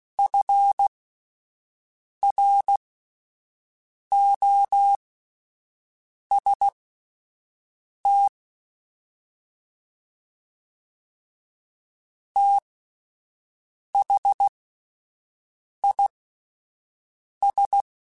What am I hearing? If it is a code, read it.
Morse code, heard as a keyed tone, about 16 wpm, spaced as for 5 wpm, 800 Hz. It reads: FROST THIS
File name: Spaced Characters / Spaced Characters-mp3s / Weather4 24 sp @10